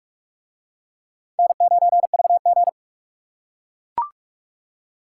Morse code, heard as a keyed tone, about 40 wpm, 700 Hz. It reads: N9VG